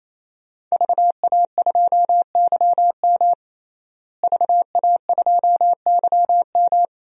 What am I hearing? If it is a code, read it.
Morse code, heard as a keyed tone, about 28 wpm, 700 Hz. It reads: VA2YM VA2YM
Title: DXCC-100 Call Signs - 5 Characters - Repeated 3x - 28wpm